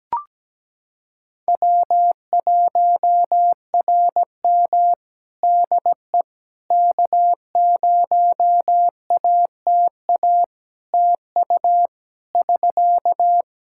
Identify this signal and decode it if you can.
Morse code, heard as a keyed tone, about 17 wpm, 700 Hz. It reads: W1RM DE K0ATA TU <SK>
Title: Practice QSO - Exchange 4 - Ending - 17wpm